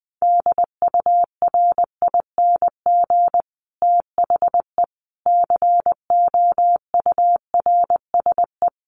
Morse code, heard as a keyed tone, about 20 wpm, 700 Hz. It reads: DURING THE COURSE
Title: Sets of 5 Words Encore 20wpm